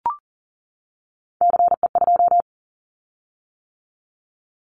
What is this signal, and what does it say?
Morse code, heard as a keyed tone, about 42 wpm, 700 Hz. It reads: CE2
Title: Callsign Prefixes for Top 100 DXCC - 40wpm